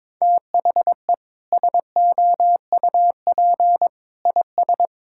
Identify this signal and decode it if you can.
Morse code, heard as a keyed tone, about 22 wpm, 700 Hz. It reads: THE SOUP IS
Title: Sentences from Top 1000 Words - Review Part 6 - 22wpm